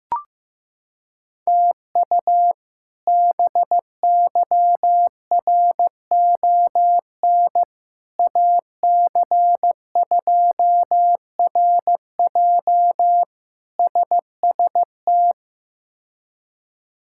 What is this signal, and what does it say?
Morse code, heard as a keyed tone, about 15 wpm, 700 Hz. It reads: TU BYRON AC2RJ SST
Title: K1USN SST QSO Practice - 15wpm